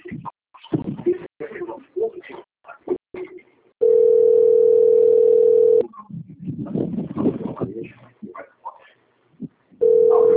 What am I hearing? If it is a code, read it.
Ringback tone (North America), 440 + 480 Hz, with speech.
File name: out- - - - - .3352